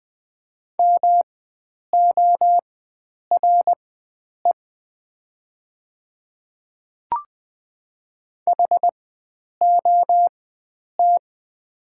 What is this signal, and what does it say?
Morse code, heard as a keyed tone, about 20 wpm, spaced as for 9 wpm, 690 Hz. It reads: MORE HOT
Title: 4x Character Spacing - Top 500 Words - 20wpm